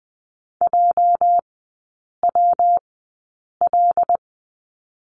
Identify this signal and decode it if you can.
Morse code, heard as a keyed tone, about 20 wpm, spaced as for 8 wpm, 700 Hz. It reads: JWL
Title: 205 Lesson 33e - Letter j - Three Characters 20wpm